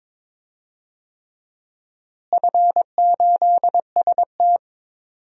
Morse code, heard as a keyed tone, about 22 wpm, 700 Hz. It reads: F8ST
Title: DXCC-100 Call Signs - 4 Characters - Repeated 3x - 22wpm